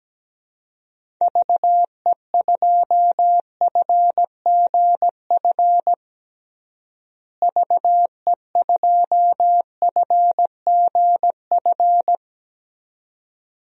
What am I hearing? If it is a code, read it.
Morse code, heard as a keyed tone, about 17 wpm, 700 Hz. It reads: VE2FGF VE2FGF
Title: DXCC-100 Call Signs - 6 Characters - Repeated 2x - 17wpm